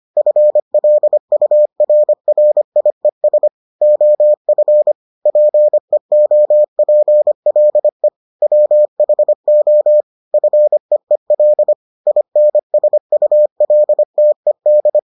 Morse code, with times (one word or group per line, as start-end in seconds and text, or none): 0.2-3.5 FLURRIES
3.8-4.9 OF
5.3-8.1 PEOPLE
8.4-10.0 WHO
10.3-11.7 FEEL
12.1-15.0 INSULTED